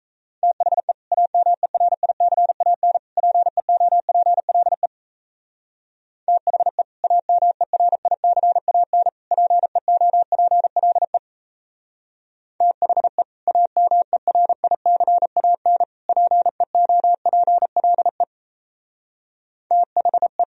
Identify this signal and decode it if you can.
Morse code, heard as a keyed tone, about 36 wpm, 700 Hz. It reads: THE AMERICAN PEOPLE THE AMERICAN PEOPLE THE AMERICAN PEOPLE THE